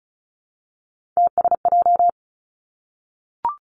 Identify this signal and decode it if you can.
Morse code, heard as a keyed tone, about 36 wpm, 700 Hz. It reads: TSJ
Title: Three Challenging Letter-Number Combinations 35wpm